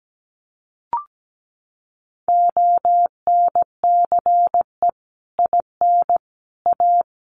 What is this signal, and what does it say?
Morse code, heard as a keyed tone, about 17 wpm, 700 Hz. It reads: ONCE IN A
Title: Sets of 4 Words 17wpm